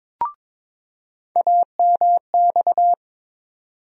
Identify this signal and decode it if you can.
Morse code, heard as a keyed tone, about 22 wpm, 700 Hz. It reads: AMX
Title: USA Call Sign Postfixes - 3 Letters 22wpm